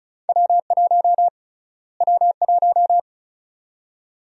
Morse code, heard as a keyed tone, about 36 wpm, 700 Hz. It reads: W1 W1
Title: USA Call Sign Prefixes - Repeated 2x - 35wpm